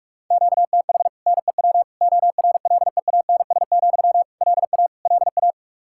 Morse code, heard as a keyed tone, about 42 wpm, 700 Hz. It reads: QTH NEW ORLEANS, LA LA
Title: Practice QSO - Exchange 1 - Beginning - 45wpm